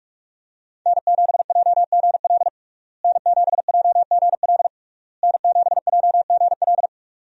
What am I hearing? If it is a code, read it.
Morse code, heard as a keyed tone, about 41 wpm, 700 Hz. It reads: N7JGL N7JGL N7JGL
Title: USA Call Signs - 5 Characters - Repeated 3x - 45wpm